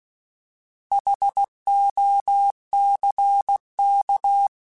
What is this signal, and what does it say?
Morse code, heard as a keyed tone, about 16 wpm, 800 Hz. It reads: HOCK
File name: Alphabetical / Alphabetical mp3s / Hs @15